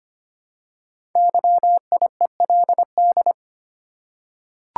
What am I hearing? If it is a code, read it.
Morse code, heard as a keyed tone, about 25 wpm, 700 Hz. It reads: YIELD
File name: 5 Letter Words - First 3 Letters Given - 25wpm